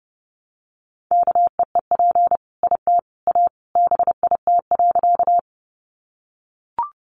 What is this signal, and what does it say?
Morse code, heard as a keyed tone, about 30 wpm, 700 Hz. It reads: KEEP IT A BIT.
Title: Sentences from Top 900 Words 30wpm